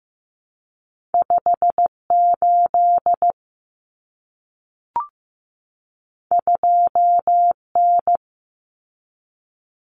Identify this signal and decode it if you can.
Morse code, heard as a keyed tone, about 15 wpm, 700 Hz. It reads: 58 2N
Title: Common Cut Numbers - 2-Digit 15wpm